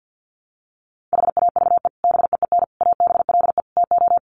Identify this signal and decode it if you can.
Morse code, heard as a keyed tone, about 50 wpm, 700 Hz. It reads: HAVE BEEN ABLE TO
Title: Sets of 4 Words 50wpm